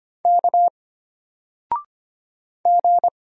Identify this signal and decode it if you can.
Morse code, heard as a keyed tone, about 25 wpm, 700 Hz. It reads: K G
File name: Single-Letter-Number-25wpm